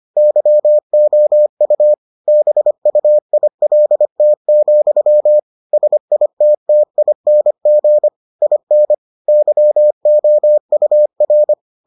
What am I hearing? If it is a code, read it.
Morse code, heard as a keyed tone, about 25 wpm, 600 Hz. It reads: YOU BUILT, SITTING IN YOUR